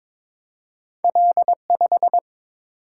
Morse code, heard as a keyed tone, about 22 wpm, 700 Hz. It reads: L5
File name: practice / 22wpm / Two Letter-Number Combinations - 22wpm